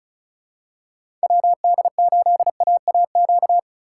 Morse code, heard as a keyed tone, about 36 wpm, 700 Hz. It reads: WD8AAQ
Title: USA Call Signs - 6 Characters - 3x Character Spacing - 35wpm